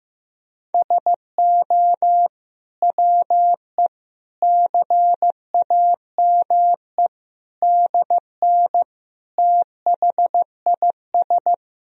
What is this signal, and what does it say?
Morse code, heard as a keyed tone, about 15 wpm, 700 Hz. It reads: SO WE CAME DN THIS
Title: Sentences from Top 400 Words and Abbreviations 15wpm